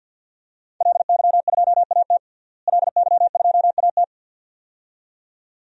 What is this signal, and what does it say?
Morse code, heard as a keyed tone, about 53 wpm, 700 Hz. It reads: LY2AT LY2AT
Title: DXCC-100 Call Signs - 5 Characters - Repeated 2x - 50wpm